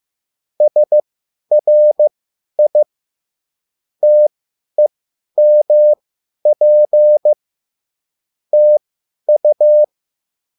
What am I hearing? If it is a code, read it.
Morse code, heard as a keyed tone, about 15 wpm, spaced as for 10 wpm, 600 Hz. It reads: SRI TEMP TU
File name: QSL / CW lesson